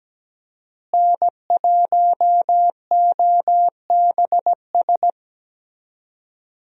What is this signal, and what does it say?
Morse code, heard as a keyed tone, about 17 wpm, 700 Hz. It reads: N1OBS
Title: USA Call Signs - 5 Characters - Repeated 3x - 17wpm